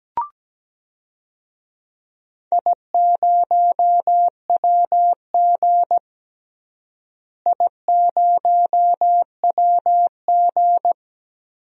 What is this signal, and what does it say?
Morse code, heard as a keyed tone, about 17 wpm, 700 Hz. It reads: I0WG I0WG
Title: DXCC-100 Call Signs - 4 Characters - Repeated 2x - 17wpm